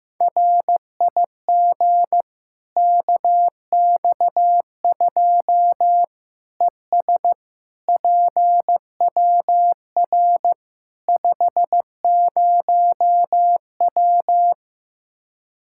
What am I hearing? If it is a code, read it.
Morse code, heard as a keyed tone, about 15 wpm, 700 Hz. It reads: RIG KX2 ES PWR 50W
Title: Practice QSO - Exchange 2 - Info - 15wpm